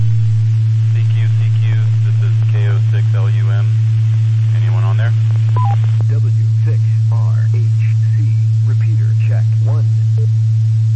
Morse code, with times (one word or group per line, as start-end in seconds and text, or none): none